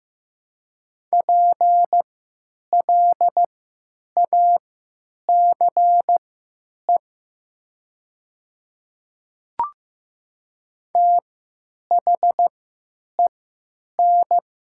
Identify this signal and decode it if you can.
Morse code, heard as a keyed tone, about 15 wpm, spaced as for 8 wpm, 700 Hz. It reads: PLACE THEN
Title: 3x Character Spacing - Top 300 Words - 15wpm